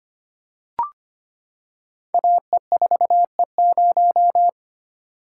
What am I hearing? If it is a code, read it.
Morse code, heard as a keyed tone, about 25 wpm, 700 Hz. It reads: AE4E0
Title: Code Groups - 5 Letter:Numbers - 25wpm